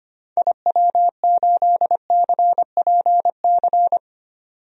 Morse code, heard as a keyed tone, about 25 wpm, 700 Hz. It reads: IW8CPC
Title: DXCC-100 Call Signs - 6 Characters - 25wpm